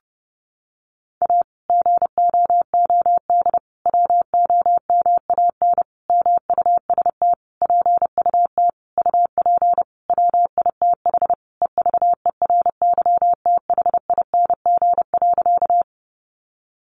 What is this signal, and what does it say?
Morse code, heard as a keyed tone, about 28 wpm, 700 Hz. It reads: A GOOD WOMAN MUST PUT UP WITH EVERYTHING.
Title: Sentences from Top 1100 Words 30wpm